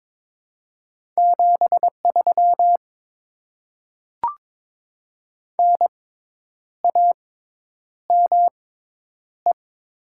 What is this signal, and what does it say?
Morse code, heard as a keyed tone, about 22 wpm, spaced as for 9 wpm, 700 Hz. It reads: 73 NAME
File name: Top 20 QSO Elements - 6x Character Spacing - 22wpm